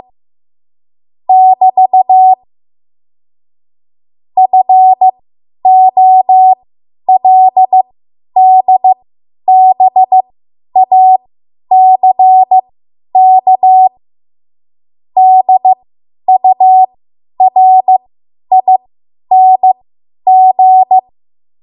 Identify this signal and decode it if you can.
Morse code, heard as a keyed tone, about 15 wpm, spaced as for 10 wpm, 750 Hz. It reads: <BT> FOLDBACK DURING
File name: Morse / 10WPM